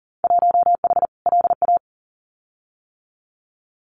Morse code, heard as a keyed tone, about 40 wpm, 700 Hz. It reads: 1H LA
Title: Winter Field Day Exchanges 40wpm